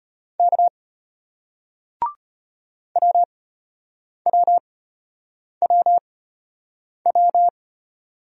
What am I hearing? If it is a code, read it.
Morse code, heard as a keyed tone, about 31 wpm, spaced as for 8 wpm, 700 Hz. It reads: K WWWW
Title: Speed-Racing Single Letters 25wpm